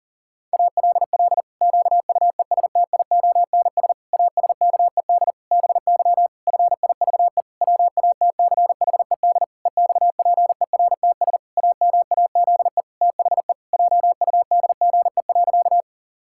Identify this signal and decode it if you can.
Morse code, heard as a keyed tone, about 40 wpm, 700 Hz. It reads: ALL QUESTIONS ASKED BY FIVE WATCHED EXPERTS AMAZE THE JUDGE.
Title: Pangram Challenge v2 - 40wpm